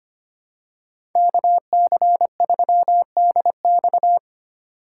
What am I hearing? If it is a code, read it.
Morse code, heard as a keyed tone, about 25 wpm, 700 Hz. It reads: KC3DX
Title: USA Call Signs - 5 Characters - 4x Character Spacing - 25wpm